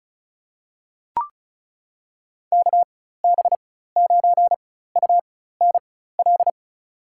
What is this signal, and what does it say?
Morse code, heard as a keyed tone, about 36 wpm, spaced as for 16 wpm, 700 Hz. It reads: KB9UNL
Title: USA Call Signs - 6 Characters - 4x Character Spacing - 35wpm